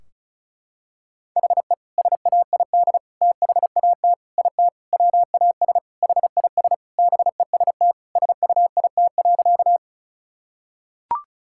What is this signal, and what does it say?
Morse code, heard as a keyed tone, about 36 wpm, 700 Hz. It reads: HE SAID THAT IT WAS HIS BEST SUIT.
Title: Sentences from Top 1600 Words 35wpm